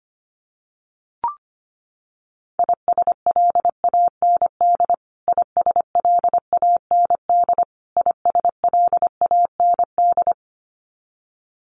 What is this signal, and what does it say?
Morse code, heard as a keyed tone, about 25 wpm, 700 Hz. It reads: ISLAND ISLAND ISLAND
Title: Top 500 Words Repeated 25wpm